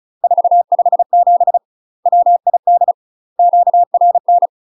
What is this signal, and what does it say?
Morse code, heard as a keyed tone, about 34 wpm, 700 Hz. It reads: 457 WID QRN
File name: UK MASTER QSOs TRAINING / 35WPM / UK MASTER QSOs 0042